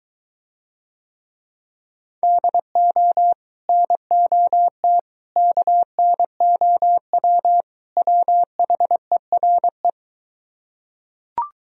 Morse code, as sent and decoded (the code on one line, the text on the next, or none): -.. --- / -. --- - / -.- -. --- .-- / .-- .... .-. .
DO NOT KNOW WHERE